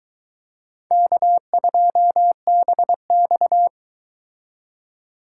Morse code, heard as a keyed tone, about 23 wpm, 700 Hz. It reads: K2BX